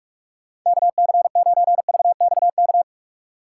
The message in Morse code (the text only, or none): KC9VXK